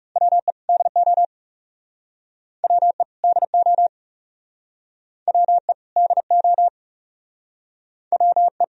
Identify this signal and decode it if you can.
Morse code, heard as a keyed tone, about 37 wpm, 700 Hz. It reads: WE DO WE DO WE DO WE
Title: Speed-Racing - Sets of 2 Words 30wpm